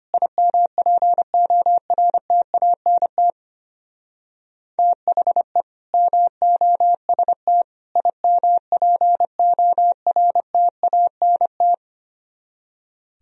Morse code, heard as a keyed tone, about 31 wpm, 700 Hz. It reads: IMPORTANT THE MOST IMPORTANT
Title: Speed-Racing - Sets of 3 Words 25wpm